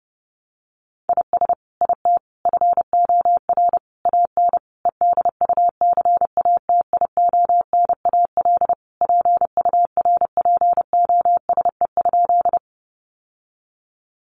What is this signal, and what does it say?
Morse code, heard as a keyed tone, about 30 wpm, 700 Hz. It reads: IS IT FOR AN EDUCATIONAL PURPOSE?
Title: Sentences from Top 1200 Words 30wpm